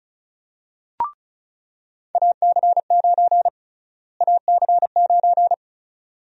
Morse code, heard as a keyed tone, about 36 wpm, 700 Hz. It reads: AC9 AC9
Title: USA Call Sign Prefixes - Repeated 2x - 35wpm